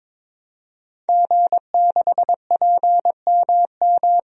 Morse code, heard as a keyed tone, about 22 wpm, 700 Hz. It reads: G6PMM
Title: DXCC-100 Call Signs - 5 Characters - Repeated 2x - 22wpm